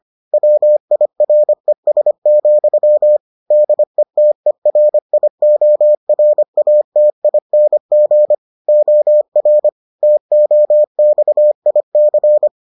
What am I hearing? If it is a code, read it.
Morse code, heard as a keyed tone, about 25 wpm, 600 Hz. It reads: WIRES, DETERIORATING OR TOXIC